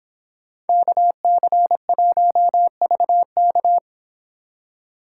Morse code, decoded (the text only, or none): KC1VK